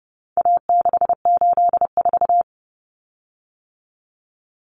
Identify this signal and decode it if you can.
Morse code, heard as a keyed tone, about 30 wpm, 700 Hz. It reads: A684
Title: Common Cut Numbers - 4-Digit 30wpm